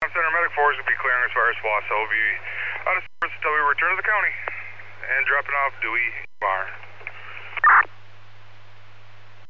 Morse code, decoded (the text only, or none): none